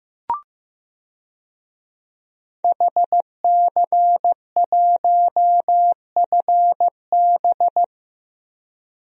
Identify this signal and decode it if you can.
Morse code, heard as a keyed tone, about 15 wpm, 700 Hz. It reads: HC1FB
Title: DXCC-100 Call Signs - 5 Characters - 15wpm